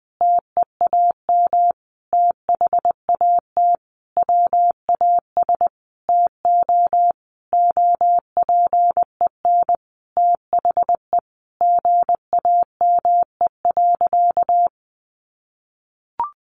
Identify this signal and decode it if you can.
Morse code, heard as a keyed tone, about 20 wpm, 700 Hz. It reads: TEAM THAT WAS TO OPEN THE GAME.
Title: Sentences from Top 1500 Words 20wpm